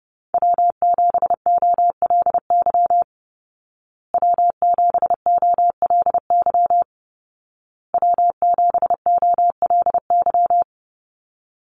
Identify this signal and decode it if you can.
Morse code, heard as a keyed tone, about 30 wpm, 700 Hz. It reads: W7OLY W7OLY W7OLY
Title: Call Signs Repeated 2x - 30wpm